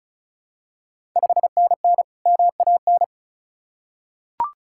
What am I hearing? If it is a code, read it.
Morse code, heard as a keyed tone, about 36 wpm, 700 Hz. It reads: 5NN MAN
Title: Ontario QSO Party Exchange 35wpm